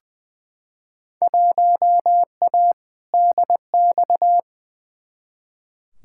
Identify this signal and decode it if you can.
Morse code, heard as a keyed tone, about 20 wpm, 700 Hz. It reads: 1A DX
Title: Field Day Exchanges Encore - 20wpm